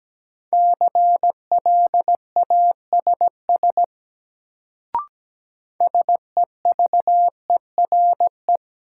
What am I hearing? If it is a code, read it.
Morse code, heard as a keyed tone, about 17 wpm, 700 Hz. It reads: CLASS SEVERE
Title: Sentences from Top 1400 Words - New Word Drill Alt - 17wpm